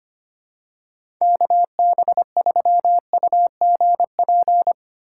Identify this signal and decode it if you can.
Morse code, heard as a keyed tone, about 25 wpm, 700 Hz. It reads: KB3UGP
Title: USA Call Signs - 6 Characters - 8x Character Spacing - 25wpm